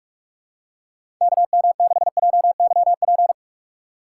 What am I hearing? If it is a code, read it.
Morse code, heard as a keyed tone, about 43 wpm, 700 Hz. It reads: KM6JYP
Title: USA Call Signs - 6 Characters - 3x Character Spacing - 45wpm